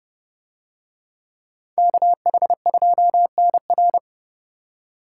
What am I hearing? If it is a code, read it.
Morse code, heard as a keyed tone, about 30 wpm, 700 Hz. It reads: KH2NR